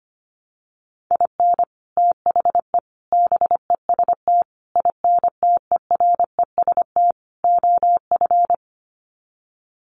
Morse code, heard as a keyed tone, about 25 wpm, 700 Hz. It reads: IN THE BEST INTEREST OF
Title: Sets of 5 Words Encore 25wpm